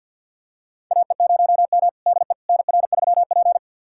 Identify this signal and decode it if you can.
Morse code, heard as a keyed tone, about 47 wpm, 700 Hz. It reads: AE0M DE NR3P
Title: USA Callsign DE Callsign - 4 Characters - 50wpm